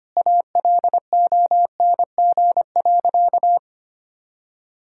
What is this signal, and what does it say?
Morse code, heard as a keyed tone, about 25 wpm, 700 Hz. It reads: ALONG.
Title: Sentences from Top 1000 Words - Review Part 1 - 25wpm